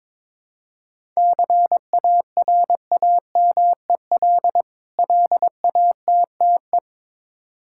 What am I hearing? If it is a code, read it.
Morse code, heard as a keyed tone, about 22 wpm, 700 Hz. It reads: CARAMEL LATTE
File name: Difficult Words with Hints - 22wpm